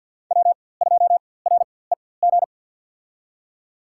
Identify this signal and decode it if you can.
Morse code, heard as a keyed tone, about 47 wpm, spaced as for 23 wpm, 700 Hz. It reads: W2REG
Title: USA Call Signs - 5 Characters - 4x Character Spacing - 50wpm